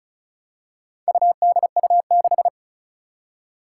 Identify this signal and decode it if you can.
Morse code, heard as a keyed tone, about 36 wpm, 700 Hz. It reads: UDU6
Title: Four Challenging Letter-Number Combinations 35wpm